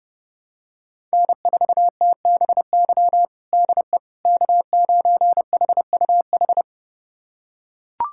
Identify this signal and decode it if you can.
Morse code, heard as a keyed tone, about 31 wpm, 700 Hz. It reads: N4TBY DE K9HUH
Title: USA Callsign DE Callsign - 5 Characters - 30wpm